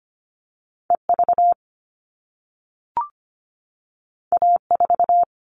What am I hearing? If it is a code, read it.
Morse code, heard as a keyed tone, about 25 wpm, 700 Hz. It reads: E V A 4
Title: All Cut Numbers - 2-Digit 25wpm